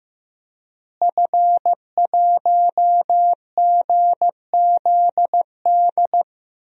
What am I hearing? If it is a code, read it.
Morse code, heard as a keyed tone, about 15 wpm, 700 Hz. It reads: F1GZD
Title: DXCC-100 Call Signs - 5 Characters - Repeated 2x - 15wpm